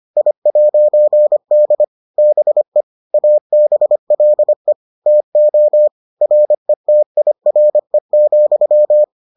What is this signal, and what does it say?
Morse code, heard as a keyed tone, about 25 wpm, 600 Hz. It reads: I'D BE ABLE TO RETIRE,